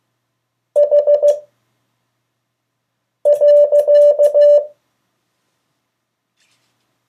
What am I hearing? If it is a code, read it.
Morse code, heard as a keyed tone, about 15 wpm, 580 Hz. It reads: H .
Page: CW Class Review – Delaware Valley Radio Association